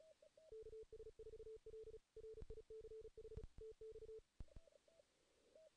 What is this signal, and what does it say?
Morse code, heard as a keyed tone, about 34 wpm, 440 Hz. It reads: KS4L RICH TX
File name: 2016 JAN CW NAQP / 2016 Jan naqp k5ot - 0-2